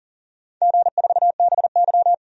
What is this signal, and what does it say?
Morse code, heard as a keyed tone, about 41 wpm, 700 Hz. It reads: G4BY
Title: DXCC-100 Call Signs - 4 Characters - 40wpm